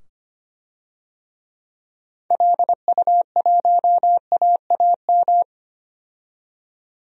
Morse code, heard as a keyed tone, about 25 wpm, 700 Hz. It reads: LU1AAM
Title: DXCC-100 Call Signs - 6 Characters - 25wpm